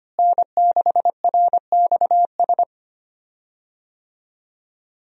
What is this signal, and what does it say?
Morse code, heard as a keyed tone, about 25 wpm, 700 Hz. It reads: N6RXS
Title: USA Call Signs - 5 Characters - Repeated 3x - 25wpm